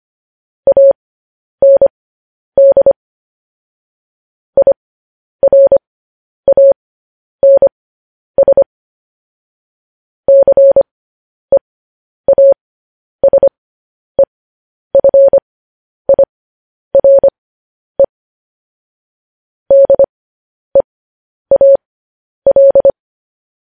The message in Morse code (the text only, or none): AND IRANS CEASEFIRE DEAL